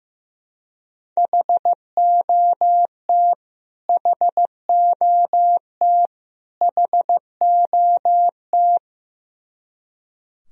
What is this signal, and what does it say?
Morse code, heard as a keyed tone, about 15 wpm, 700 Hz. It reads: HOT HOT HOT
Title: Top 100 Words Repeated 15wpm